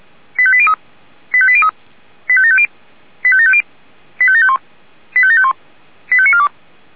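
Five-tone selective call (ZVEI), heard as quick runs of five tones, tones of about 70 ms. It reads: 85803 85803 85750 85750 85751 85751 85942
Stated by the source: Received by SDR.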